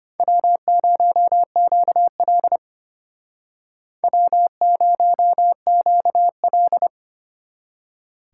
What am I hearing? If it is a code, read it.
Morse code, heard as a keyed tone, about 32 wpm, 700 Hz. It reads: W0QL W0QL